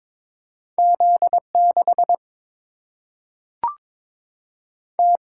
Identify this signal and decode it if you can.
Morse code, heard as a keyed tone, about 22 wpm, 700 Hz. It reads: Z 6 T
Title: Single-Letter Single-Number 22wpm